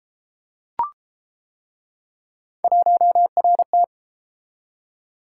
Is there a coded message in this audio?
Yes: Morse code from 2.6 to 3.8 s, heard as a keyed tone, about 31 wpm, 700 Hz. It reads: 1RT